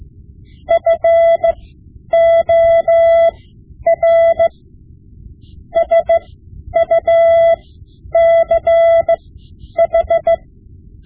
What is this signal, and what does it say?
Morse code, heard as a keyed tone, about 16 wpm, spaced as for 10 wpm, 670 Hz. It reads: FOR SUCH
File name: Morse Lessons / Morse Lessons / Tape-5B